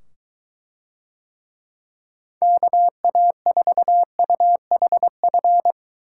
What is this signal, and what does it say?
Morse code, heard as a keyed tone, about 23 wpm, 700 Hz. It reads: KA4UHF